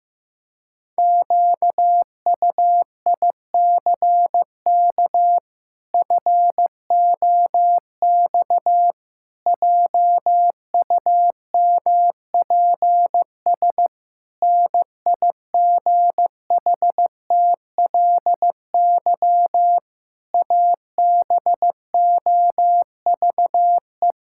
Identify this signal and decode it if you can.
Morse code, heard as a keyed tone, about 15 wpm, 700 Hz. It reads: QUICK FOX JUMPS NIGHTLY ABOVE
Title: Pangram Challenge v2 - 15wpm